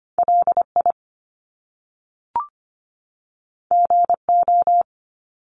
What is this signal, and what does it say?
Morse code, heard as a keyed tone, about 25 wpm, 700 Hz. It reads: LI GO
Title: Top Two Letter Combinations in Words 25wpm